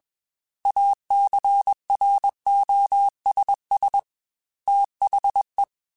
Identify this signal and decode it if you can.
Morse code, heard as a keyed tone, about 21 wpm, 800 Hz. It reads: ACROSS THE
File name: Technical Subjects / Technical Subjects-mp3s / XC @20